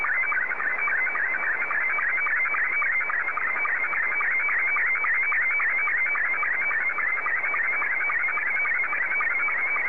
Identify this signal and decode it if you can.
SSTV (slow-scan television), Scottie 2.